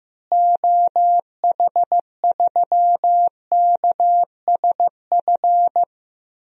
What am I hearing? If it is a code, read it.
Morse code, heard as a keyed tone, about 15 wpm, 700 Hz. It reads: OH3KSF
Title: DXCC-100 Call Signs - 6 Characters - Repeated 2x - 15wpm